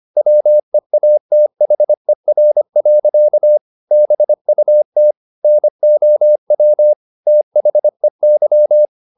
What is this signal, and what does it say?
Morse code, heard as a keyed tone, about 25 wpm, 600 Hz. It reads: WEATHER. BUT NOW THEY
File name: the old man and the sea part 1 25wpm